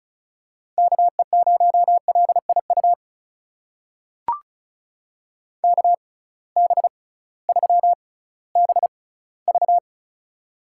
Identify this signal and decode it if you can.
Morse code, heard as a keyed tone, about 36 wpm, spaced as for 19 wpm, 700 Hz. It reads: KE0LIU KB3BV
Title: USA Call Signs - 6x Character Spacing - 35wpm